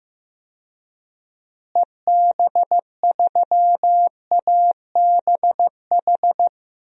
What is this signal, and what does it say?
Morse code, heard as a keyed tone, about 15 wpm, 700 Hz. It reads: EB3ABH
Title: DXCC-100 Call Signs - 6 Characters - Repeated 3x - 15wpm